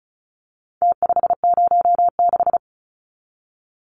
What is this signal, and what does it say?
Morse code, heard as a keyed tone, about 34 wpm, 700 Hz. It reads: T506
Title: Numbers - Contest Style - 4-Digit 35wpm